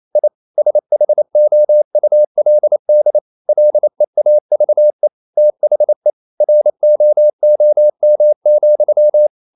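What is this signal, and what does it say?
Morse code, heard as a keyed tone, about 28 wpm, 600 Hz. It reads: I SHOULD LEAVE THE ROOM,
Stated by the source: Morse code practice at 28 wpm, text of Ken Follett's novel Never in English, from CW med Gnister.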